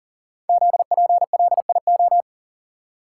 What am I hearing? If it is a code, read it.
Morse code, heard as a keyed tone, about 39 wpm, 700 Hz. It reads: ZPLIO
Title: Code Groups - 5 Letters - 40wpm